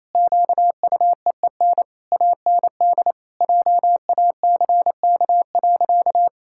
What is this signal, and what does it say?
Morse code, heard as a keyed tone, about 28 wpm, 700 Hz. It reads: QUEEN AND JACK.
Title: Pangram Challenge v2 - 28wpm